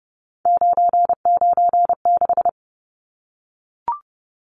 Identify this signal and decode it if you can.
Morse code, heard as a keyed tone, about 30 wpm, 700 Hz. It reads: 996
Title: Numbers - 3-Digit 30wpm